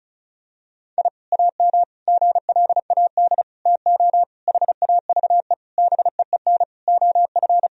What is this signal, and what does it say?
Morse code, heard as a keyed tone, about 35 wpm, 700 Hz. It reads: I AM GLAD TO HAVE BEEN OF
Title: Sentences from Top 1000 Words - Review Part 4 - 35wpm